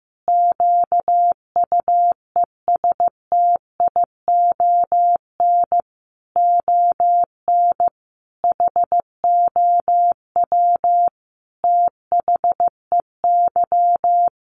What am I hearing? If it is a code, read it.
Morse code, heard as a keyed tone, about 15 wpm, 700 Hz. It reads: QUESTION ON HOW THEY
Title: Sentences from Top 1300 Words 15wpm